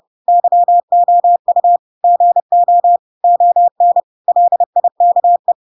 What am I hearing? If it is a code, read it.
Morse code, heard as a keyed tone, about 30 wpm, 700 Hz. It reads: YOU GO ON LIKE
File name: Pinocchio / Pinocchio ch4 30 30 ik1hge